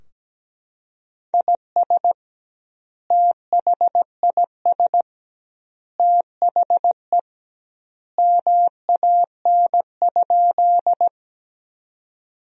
Morse code, heard as a keyed tone, about 17 wpm, 700 Hz. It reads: IS THIS THE MAN?
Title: 2x Word-Spacing - Sentences from Top 200 Words - 17wpm